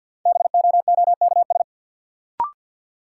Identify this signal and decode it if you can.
Morse code, heard as a keyed tone, about 52 wpm, 700 Hz. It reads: BOOKS
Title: Sentences from Top 1000 Words - Review Part 4 - New Word Drill - 50wpm